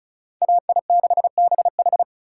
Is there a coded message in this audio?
Yes: Morse code from 0.4 to 2.0 s, heard as a keyed tone, about 45 wpm, 700 Hz. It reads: AI6BH